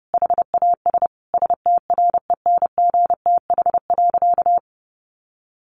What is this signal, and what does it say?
Morse code, heard as a keyed tone, about 30 wpm, 700 Hz. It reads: HAS STRENGTH.
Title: Sentences from Top 1300 Words 30wpm